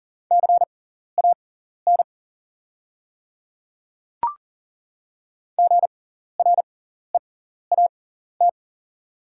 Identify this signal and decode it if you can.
Morse code, heard as a keyed tone, about 40 wpm, spaced as for 14 wpm, 700 Hz. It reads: CAN GREAT
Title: 6x Character Spacing - Top 200 Words - 40wpm